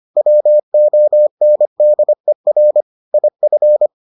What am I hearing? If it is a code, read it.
Morse code, heard as a keyed tone, about 25 wpm, 600 Hz. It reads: WONDER IF